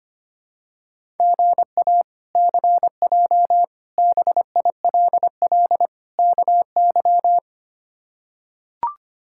Morse code, heard as a keyed tone, about 25 wpm, 700 Hz. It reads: GA CJ BILL KY
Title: K1USN SST QSO Practice - 25wpm